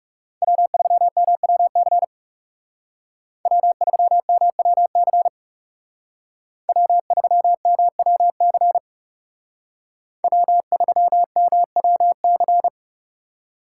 Morse code, heard as a keyed tone, about 36 wpm, 700 Hz. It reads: W3MWC W3MWC W3MWC W3MWC
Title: Speed Racing - USA 1x3 Call Signs 30wpm